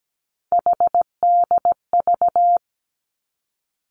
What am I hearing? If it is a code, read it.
Morse code, heard as a keyed tone, about 17 wpm, 700 Hz. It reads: HDV
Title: Three Challenging Letter-Number Combinations 17wpm